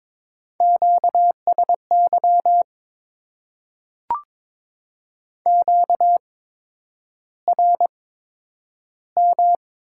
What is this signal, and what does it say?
Morse code, heard as a keyed tone, about 22 wpm, spaced as for 9 wpm, 690 Hz. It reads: QSY QRM
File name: Top 90 QSO Elements - 8x Character Spacing - 22wpm